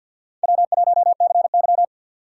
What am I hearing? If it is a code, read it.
Morse code, heard as a keyed tone, about 47 wpm, 700 Hz. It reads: W1CY
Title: USA Call Signs - 4 Characters - 6x Character Spacing - 50wpm